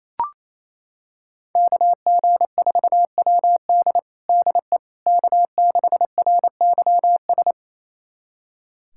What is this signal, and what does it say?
Morse code, heard as a keyed tone, about 28 wpm, 700 Hz. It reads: KG4WD DE K6RYS
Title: USA Callsign DE Callsign - 5 Characters - 28wpm